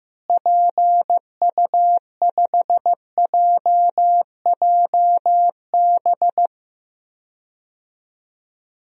Morse code, heard as a keyed tone, about 16 wpm, 700 Hz. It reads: PU5JJB
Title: DXCC-100 Call Signs - 6 Characters - Repeated 3x - 15wpm